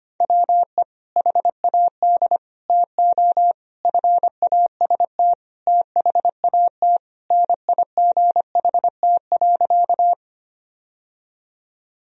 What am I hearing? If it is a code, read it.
Morse code, heard as a keyed tone, about 25 wpm, 700 Hz. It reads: WE HAD TO FAST THAT NIGHT.
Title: Sentences from Top 1000 Words - Review Part 2 - 25wpm